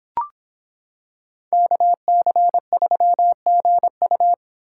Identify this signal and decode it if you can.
Morse code, heard as a keyed tone, about 25 wpm, 700 Hz. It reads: KC3GU